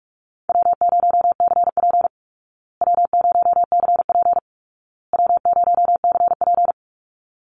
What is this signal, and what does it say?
Morse code, heard as a keyed tone, about 44 wpm, 700 Hz. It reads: W0CP W0CP W0CP
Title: Easy Call Signs Repeated 3x - 45wpm